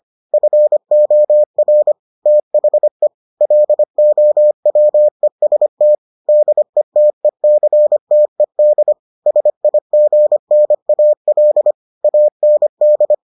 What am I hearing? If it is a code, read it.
Morse code, heard as a keyed tone, about 25 wpm, 600 Hz. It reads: FOR THE LOWEST DETECTED SIGNAL AND